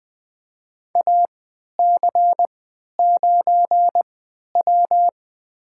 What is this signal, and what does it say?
Morse code, heard as a keyed tone, about 20 wpm, spaced as for 11 wpm, 700 Hz. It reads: AC9W
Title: USA Call Signs - 4 Characters - 3x Character Spacing - 20wpm